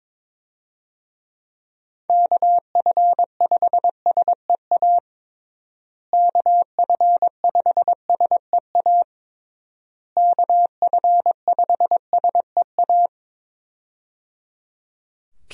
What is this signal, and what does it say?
Morse code, heard as a keyed tone, about 22 wpm, 700 Hz. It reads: KF5SEA KF5SEA KF5SEA
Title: USA Call Signs - 6 Characters - Repeated 3x - 22wpm